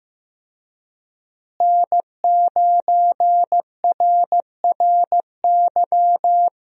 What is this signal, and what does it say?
Morse code, heard as a keyed tone, about 15 wpm, 700 Hz. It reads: N9RRY
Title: USA Call Signs - 5 Characters - Repeated 2x - 15wpm